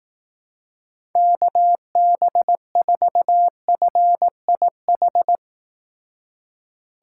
Morse code, heard as a keyed tone, about 18 wpm, 700 Hz. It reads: KB4FIH
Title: Speed Racing - USA 2x3 Call Signs 15wpm